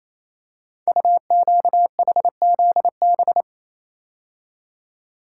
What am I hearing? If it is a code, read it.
Morse code, heard as a keyed tone, about 27 wpm, 700 Hz. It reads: UQHZB
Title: Code Groups - 5 Letters - 28wpm